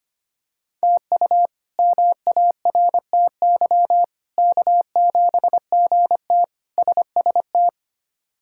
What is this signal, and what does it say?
Morse code, heard as a keyed tone, about 25 wpm, 700 Hz. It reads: TU MARTY K7GT SST
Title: K1USN SST QSO Practice - 25wpm